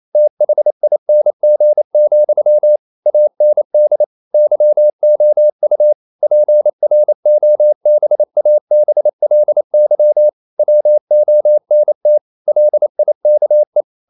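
Morse code, heard as a keyed tone, about 28 wpm, 600 Hz. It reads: THING, AND YOU PROBABLY WONT LIKE
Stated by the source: Morse code practice at 28 wpm, text of Ken Follett's novel Never in English, from CW med Gnister.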